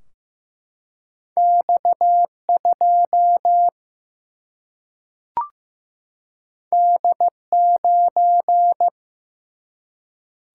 Morse code, heard as a keyed tone, about 15 wpm, 700 Hz. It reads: X 2 D 9
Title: Single-Letter Single-Number 15wpm